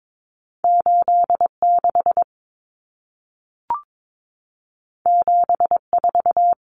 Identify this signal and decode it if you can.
Morse code, heard as a keyed tone, about 22 wpm, 700 Hz. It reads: 86 74
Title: Common Cut Numbers - 2-Digit 22wpm